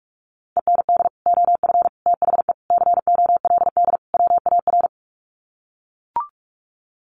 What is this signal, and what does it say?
Morse code, heard as a keyed tone, about 44 wpm, 700 Hz. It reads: END OF THE COLD WAR
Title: Sets of 5 Words Encore 45wpm